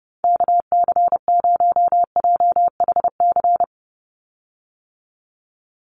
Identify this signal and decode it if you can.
Morse code, heard as a keyed tone, about 30 wpm, 700 Hz. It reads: KC0JHC
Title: USA Call Signs 30wpm